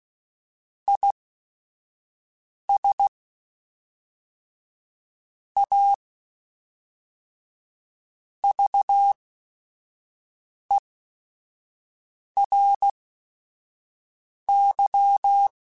Morse code, heard as a keyed tone, about 16 wpm, spaced as for 5 wpm, 800 Hz. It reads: IS A VERY
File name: Technical Subjects / Technical Subjects-mp3s / Reactance @5